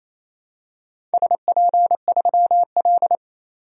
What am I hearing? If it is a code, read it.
Morse code, heard as a keyed tone, about 28 wpm, 700 Hz. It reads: SP3L
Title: DXCC-100 Call Signs - 4 Characters - 28wpm